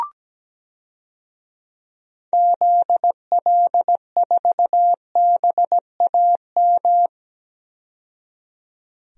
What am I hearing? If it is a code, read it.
Morse code, heard as a keyed tone, about 17 wpm, 700 Hz. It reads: ZL4BAM
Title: DXCC-100 Call Signs - 6 Characters - 17wpm